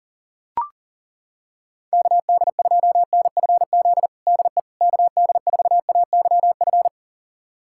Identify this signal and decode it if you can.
Morse code, heard as a keyed tone, about 40 wpm, 700 Hz. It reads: KD2NFZ DE KD4AYF
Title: USA Callsign DE Callsign - 6 Characters - 40wpm